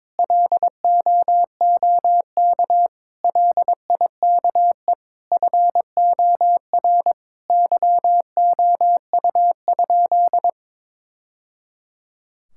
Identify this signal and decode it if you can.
Morse code, heard as a keyed tone, about 22 wpm, 700 Hz. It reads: LOOK LIKE FOR YOU?
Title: Sentences from Top 1000 Words - Review Part 6 - 22wpm